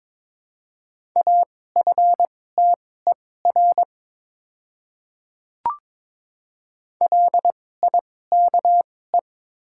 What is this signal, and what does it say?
Morse code, heard as a keyed tone, about 22 wpm, spaced as for 16 wpm, 700 Hz. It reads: AFTER LIKE
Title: 2x Character Spacing - Top 300 Words - 22wpm